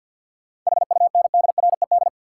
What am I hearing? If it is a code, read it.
Morse code, heard as a keyed tone, about 52 wpm, 700 Hz. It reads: HUNDRED
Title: Sentences from Top 400 Words - New Word Drill - 50wpm